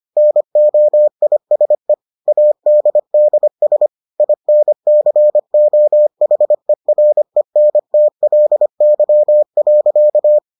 Morse code, heard as a keyed tone, about 25 wpm, 600 Hz. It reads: NOISE ADDS INCOHERENTLY.